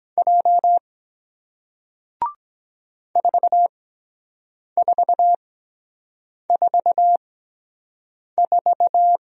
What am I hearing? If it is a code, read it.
Morse code, heard as a keyed tone, about 23 wpm, spaced as for 7 wpm, 700 Hz. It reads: J 4444